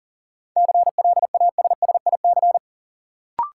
Morse code, heard as a keyed tone, about 40 wpm, 700 Hz. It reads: CLASSIC
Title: Sentences from Top 1000 Words - New Word Drill - 40wpm